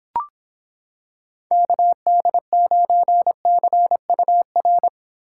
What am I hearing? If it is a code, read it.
Morse code, heard as a keyed tone, about 26 wpm, 700 Hz. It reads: KD9CUR